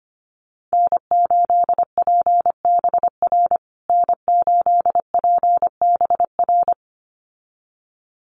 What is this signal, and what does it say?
Morse code, heard as a keyed tone, about 25 wpm, 700 Hz. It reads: N8PBR N8PBR
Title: USA Call Signs Repeated 2x 25wpm